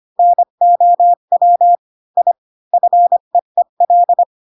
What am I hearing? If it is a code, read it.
Morse code, heard as a keyed tone, about 25 wpm, 700 Hz. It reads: NOW I FEEL
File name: Pinocchio / Pinocchio ch7 20 25 ik1hge